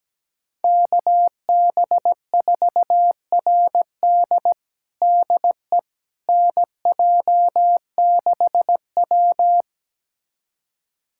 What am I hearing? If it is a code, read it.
Morse code, heard as a keyed tone, about 17 wpm, 700 Hz. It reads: KB4RD DE NJ6W
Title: USA Callsign DE Callsign 17wpm